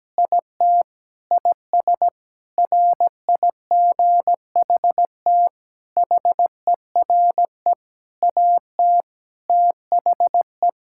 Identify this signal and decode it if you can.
Morse code, heard as a keyed tone, about 17 wpm, 700 Hz. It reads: IT IS RIGHT HERE AT THE
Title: Sentences from Top 1600 Words 17wpm